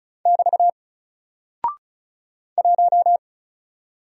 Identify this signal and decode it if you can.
Morse code, heard as a keyed tone, about 36 wpm, 700 Hz. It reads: <BT> 1
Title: Single Character (Letter:Number:Punctuation:Prosign) 35wpm